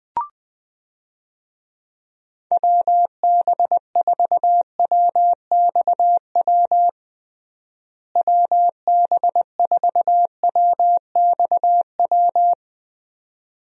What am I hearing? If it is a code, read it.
Morse code, heard as a keyed tone, about 21 wpm, 700 Hz. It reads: WB4WXW WB4WXW
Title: USA Call Signs - 6 Characters - Repeated 2x - 20wpm